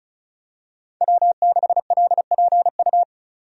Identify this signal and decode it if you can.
Morse code, heard as a keyed tone, about 36 wpm, 700 Hz. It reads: W6LPU
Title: USA Call Signs - 5 Characters - 35wpm